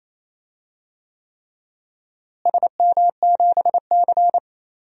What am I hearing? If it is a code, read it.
Morse code, heard as a keyed tone, about 28 wpm, 700 Hz. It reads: SM7C
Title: DXCC-100 Call Signs - 4 Characters - Repeated 3x - 28wpm